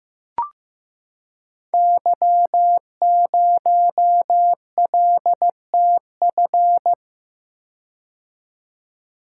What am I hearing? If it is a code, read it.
Morse code, heard as a keyed tone, about 15 wpm, 700 Hz. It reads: Y0LTF
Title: Code Groups - 5 Letter:Numbers - 15wpm